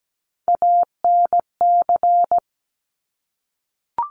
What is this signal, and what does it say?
Morse code, heard as a keyed tone, about 17 wpm, 700 Hz. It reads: ANC
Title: Top Three Letter Combinations in Words 17wpm